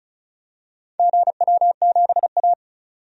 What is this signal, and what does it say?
Morse code, heard as a keyed tone, about 36 wpm, 700 Hz. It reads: GW7A
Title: DXCC-100 Call Signs - 4 Characters - 35wpm